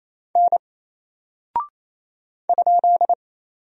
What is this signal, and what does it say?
Morse code, heard as a keyed tone, about 27 wpm, 700 Hz. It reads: N ?
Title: Single Character (Letter:Number:Punctuation:Prosign) 28wpm